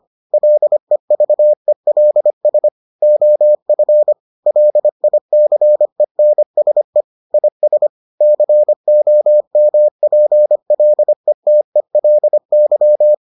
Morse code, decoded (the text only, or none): LEVELS OF LICENSE IS COMPLETELY